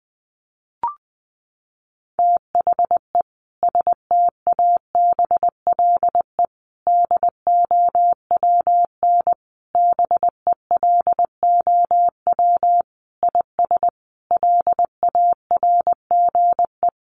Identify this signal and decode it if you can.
Morse code, heard as a keyed tone, about 20 wpm, 700 Hz. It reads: THE STABLE DOWN BELOW IS LARGE
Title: Sentences from Top 1500 Words 20wpm